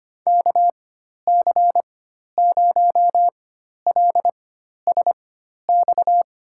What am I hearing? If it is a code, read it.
Morse code, heard as a keyed tone, about 25 wpm, spaced as for 12 wpm, 700 Hz. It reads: KC0LSX
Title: USA Call Signs - 4x Character Spacing - 25wpm